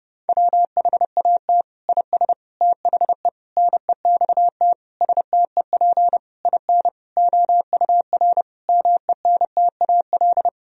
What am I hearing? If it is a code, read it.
Morse code, heard as a keyed tone, about 32 wpm, 700 Hz. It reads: WHAT IS THE NEXT STEP IN OUR MENTAL
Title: Sentences from Top 1000 Words - Review Part 4 - 30wpm